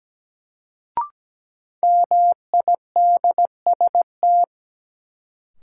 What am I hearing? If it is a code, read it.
Morse code, heard as a keyed tone, about 17 wpm, 700 Hz. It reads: MIDST
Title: Sets of 4 Words - New Word Drill - 17wpm